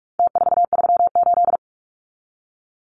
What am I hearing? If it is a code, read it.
Morse code, heard as a keyed tone, about 44 wpm, 700 Hz. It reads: T438
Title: Numbers - Contest Style - 4-Digit 45wpm